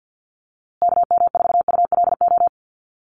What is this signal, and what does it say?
Morse code, heard as a keyed tone, about 50 wpm, 700 Hz. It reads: KM4ULO
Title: USA Call Signs Repeated 3x 50wpm